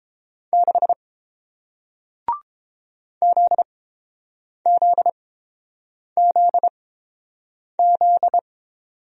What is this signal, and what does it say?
Morse code, heard as a keyed tone, about 30 wpm, spaced as for 7 wpm, 690 Hz. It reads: 6 ZZZZ